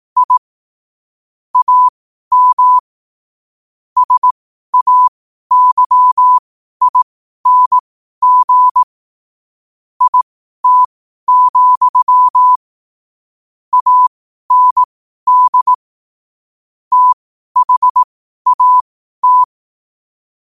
Morse code, heard as a keyed tone, about 18 wpm, spaced as for 12 wpm, 1000 Hz. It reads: I AM SAYING IT, AND THAT
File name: qotdfiles / 12wpm